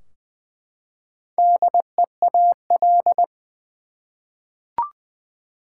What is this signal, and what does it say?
Morse code, heard as a keyed tone, about 21 wpm, 700 Hz. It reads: DEAL
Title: Four Letter Words - 20wpm